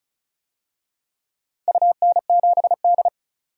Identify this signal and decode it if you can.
Morse code, heard as a keyed tone, about 36 wpm, 700 Hz. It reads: UN7D
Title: DXCC-100 Call Signs - 4 Characters - 35wpm